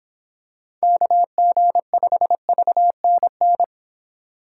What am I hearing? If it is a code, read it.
Morse code, heard as a keyed tone, about 26 wpm, 700 Hz. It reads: KG5VNN